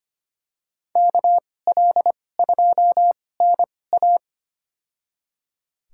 Morse code, heard as a keyed tone, about 25 wpm, spaced as for 18 wpm, 700 Hz. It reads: KL2NA
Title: USA Call Signs - 2x Character Spacing - 25wpm